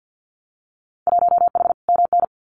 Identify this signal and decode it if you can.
Morse code, heard as a keyed tone, about 50 wpm, 700 Hz. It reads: 1H MN
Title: Winter Field Day Exchanges 50wpm